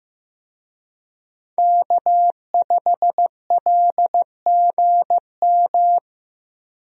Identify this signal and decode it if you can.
Morse code, heard as a keyed tone, about 15 wpm, 700 Hz. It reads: K5LGM
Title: USA Call Signs - 5 Characters - Repeated 2x - 15wpm